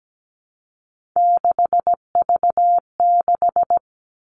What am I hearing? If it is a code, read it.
Morse code, heard as a keyed tone, about 17 wpm, 700 Hz. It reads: 6V6
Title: Three Challenging Letter-Number Combinations 17wpm